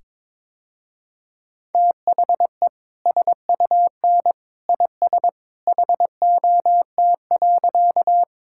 Morse code, heard as a keyed tone, about 22 wpm, 700 Hz. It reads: THE SUN IS HOT.
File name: practice / 22wpm / Easy Sentences with Hints - 22wpm